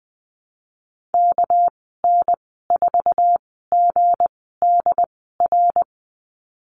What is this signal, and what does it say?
Morse code, heard as a keyed tone, about 20 wpm, spaced as for 15 wpm, 700 Hz. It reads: KN4GDR
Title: 238 Lesson 38g - Letter q - Call Signs 20wpm